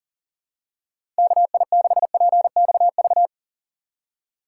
Morse code, heard as a keyed tone, about 40 wpm, 700 Hz. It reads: KI6PXV
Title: USA Call Signs - 6 Characters - 2x Character Spacing - 40wpm